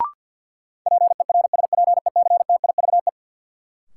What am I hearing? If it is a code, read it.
Morse code, heard as a keyed tone, about 53 wpm, 700 Hz. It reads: PERSPECTIVE
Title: Sentences from Top 1200 Words - New Word Drill Alt - 50wpm